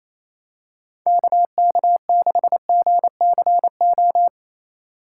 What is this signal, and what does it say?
Morse code, heard as a keyed tone, about 28 wpm, 700 Hz. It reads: KK6GCO
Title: USA Call Signs - 6 Characters - 4x Character Spacing - 28wpm